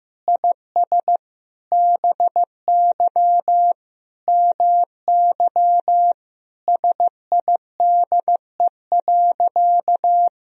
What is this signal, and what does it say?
Morse code, heard as a keyed tone, about 16 wpm, 700 Hz. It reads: IS BY MY SIDE.
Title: Sentences from Top 1000 Words - Review Part 1 - 15wpm